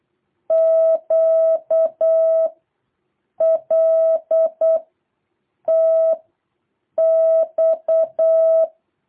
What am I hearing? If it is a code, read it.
Morse code, heard as a keyed tone, about 8 wpm, 650 Hz. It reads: Q L T X
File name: jh8cbh / 2023cw